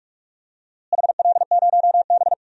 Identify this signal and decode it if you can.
Morse code, heard as a keyed tone, about 43 wpm, 700 Hz. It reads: HL0B
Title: DXCC-100 Call Signs - 4 Characters - Repeated 2x - 45wpm